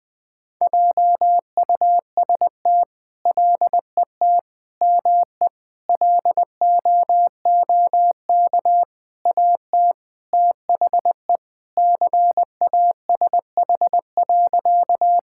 Morse code, heard as a keyed tone, about 20 wpm, 700 Hz. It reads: JUST LET ME LOOK AT THE CASH.
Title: Sentences from Top 1000 Words - Review Part 5 - 20wpm